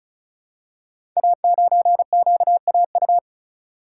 Morse code, heard as a keyed tone, about 36 wpm, 700 Hz. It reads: A9QAU
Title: Code Groups - 5 Letter:Numbers - 35wpm